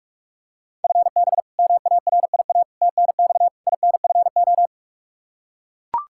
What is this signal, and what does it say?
Morse code, heard as a keyed tone, about 44 wpm, 700 Hz. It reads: FB MARIA TNX INFO
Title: K1USN SST QSO Practice 45wpm